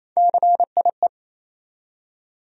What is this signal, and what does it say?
Morse code, heard as a keyed tone, about 28 wpm, 700 Hz. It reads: CIE
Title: USA Call Sign Postfixes - 3 Letters 28wpm